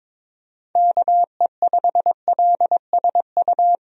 Morse code, heard as a keyed tone, about 21 wpm, 700 Hz. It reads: KE5LSU
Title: USA Call Signs - 6 Characters - Repeated 2x - 22wpm